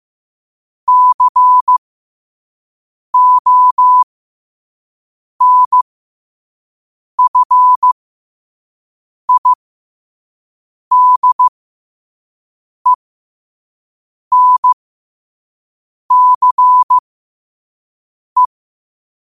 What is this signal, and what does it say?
Morse code, heard as a keyed tone, about 15 wpm, spaced as for 5 wpm, 1000 Hz. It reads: CONFIDENCE
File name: qotdfiles / 5wpm